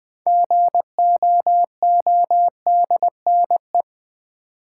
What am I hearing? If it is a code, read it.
Morse code, heard as a keyed tone, about 20 wpm, 700 Hz. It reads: GOODNE
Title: Build A Word - 2 Letters At A Time - 8-Letter Words - 20wpm